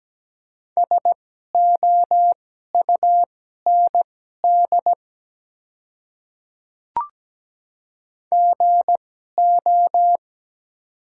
Morse code, heard as a keyed tone, about 17 wpm, spaced as for 12 wpm, 690 Hz. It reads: SOUND GO
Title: 2x Character Spacing - Top 100 Words - 17wpm